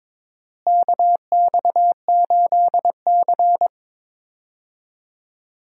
Morse code, heard as a keyed tone, about 22 wpm, 700 Hz. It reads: KX8C
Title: USA Call Signs - 4 Characters - Repeated 2x - 22wpm